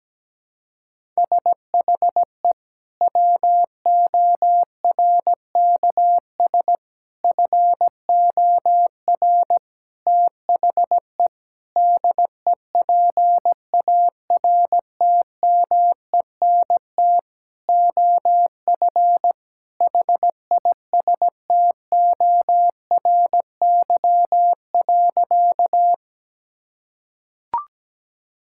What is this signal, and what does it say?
Morse code, heard as a keyed tone, about 17 wpm, 700 Hz. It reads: SHE WORKS FOR THE DEPARTMENT OF HISTORY.
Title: Sentences from Top 1000 Words - Review Part 5 - 17wpm